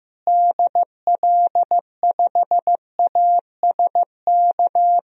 Morse code, heard as a keyed tone, about 15 wpm, 700 Hz. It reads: DL5ASK
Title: DXCC-100 Call Signs - 6 Characters - Repeated 3x - 15wpm